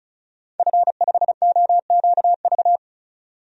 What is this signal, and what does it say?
Morse code, heard as a keyed tone, about 34 wpm, 700 Hz. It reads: F5OQV
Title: DXCC-100 Call Signs - 5 Characters - 35wpm